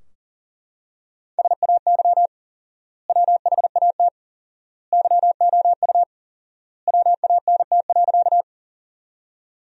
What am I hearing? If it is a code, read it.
Morse code, heard as a keyed tone, about 40 wpm, 700 Hz. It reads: SAY WHAT YOU WANT.
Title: 4x Word-Spacing - Sentences from Top 200 Words - 40wpm